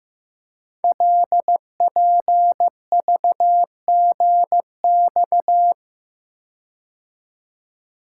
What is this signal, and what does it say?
Morse code, heard as a keyed tone, about 15 wpm, 700 Hz. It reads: LPVGX
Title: Code Groups - 5 Letters - 15wpm